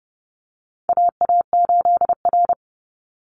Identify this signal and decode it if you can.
Morse code, heard as a keyed tone, about 30 wpm, 700 Hz. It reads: AA8R
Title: Call Signs Repeated 3x - 30wpm